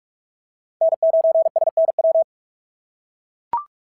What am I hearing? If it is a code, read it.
Morse code, heard as a keyed tone, about 44 wpm, 690 Hz. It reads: N9SNW
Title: SOTA QSO Practice 45wpm